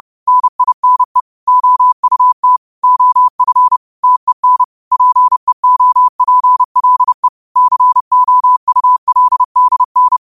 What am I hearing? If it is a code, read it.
Morse code, heard as a keyed tone, about 30 wpm, 1000 Hz. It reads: NINE OUT OF TEN PEOPLE COULDN